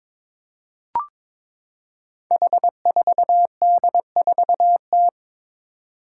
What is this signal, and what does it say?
Morse code, heard as a keyed tone, about 22 wpm, 700 Hz. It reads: H4D4T
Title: Five Challenging Letter-Number Combinations 22wpm